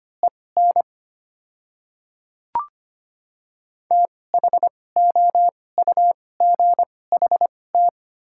Morse code, heard as a keyed tone, about 25 wpm, spaced as for 18 wpm, 700 Hz. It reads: EN THOUGHT
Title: 2x Character Spacing - Top 500 Words - 25wpm